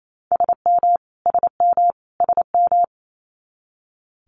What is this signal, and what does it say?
Morse code, heard as a keyed tone, about 28 wpm, 700 Hz. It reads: SM SM SM
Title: Multi-letter Phonemes Repeated 3x 28wpm